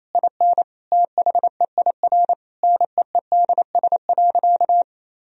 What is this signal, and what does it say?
Morse code, heard as a keyed tone, about 28 wpm, 700 Hz. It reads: IN THEIR NEEDS.
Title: Sentences from Top 1000 Words - Review Part 3 - 28wpm